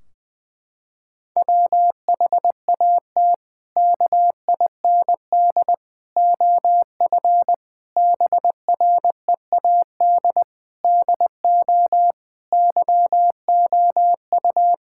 Morse code, heard as a keyed tone, about 20 wpm, 700 Hz. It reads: WHAT KIND OF BREAD DO YOU
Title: Sentences from Top 1600 Words 20wpm